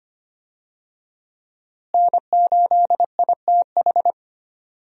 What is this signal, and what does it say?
Morse code, heard as a keyed tone, about 25 wpm, 700 Hz. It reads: N8ITH
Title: USA Call Signs - 5 Characters - Repeated 3x - 25wpm